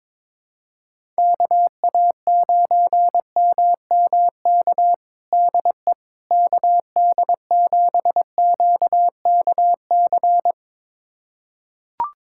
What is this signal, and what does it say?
Morse code, heard as a keyed tone, about 22 wpm, 700 Hz. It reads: KA9MMK DE KD7QKC
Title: USA Callsign DE Callsign 22wpm